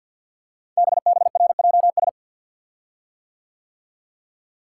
Morse code, heard as a keyed tone, about 50 wpm, 700 Hz. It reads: BBRJS
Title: Code Groups - 5 Letters - 50wpm